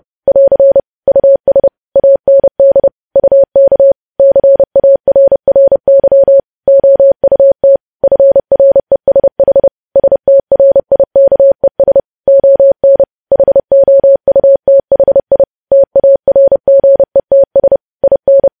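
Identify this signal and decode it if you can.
Morse code, heard as a keyed tone, about 30 wpm, 550 Hz. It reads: <AR> US AND UK CARRY OUT FRESH STRIKES ON HOUTHI TARGETS IN
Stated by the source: News headlines sent at 30 wpm.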